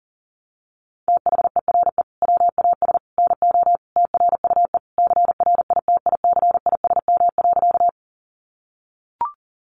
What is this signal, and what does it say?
Morse code, heard as a keyed tone, about 40 wpm, 700 Hz. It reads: THERE WAS NO TRUE CRITICISM.
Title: Sentences from Top 1300 Words 40wpm